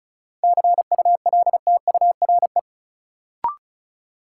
Morse code, heard as a keyed tone, about 36 wpm, 700 Hz. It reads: CULTURE
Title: Sentences from Top 1000 Words - Review Part 6 - New Word Drill - 35wpm